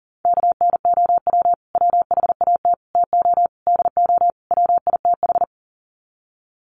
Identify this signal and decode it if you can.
Morse code, heard as a keyed tone, about 37 wpm, 700 Hz. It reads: KNOW WHAT TO DO WITH
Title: Sets of 5 Words 40wpm